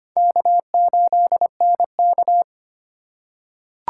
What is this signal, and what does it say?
Morse code, heard as a keyed tone, about 25 wpm, 700 Hz. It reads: K8NK
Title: USA Call Signs - 4 Characters - 4x Character Spacing - 25wpm